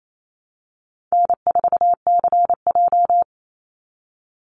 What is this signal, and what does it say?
Morse code, heard as a keyed tone, about 28 wpm, 700 Hz. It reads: N4CJ
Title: Easy Call Signs - 28wpm